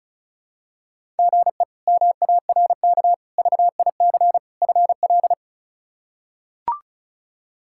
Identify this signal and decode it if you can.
Morse code, heard as a keyed tone, about 36 wpm, 700 Hz. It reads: GE MARK VIC FL
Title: K1USN SST QSO Practice - 35wpm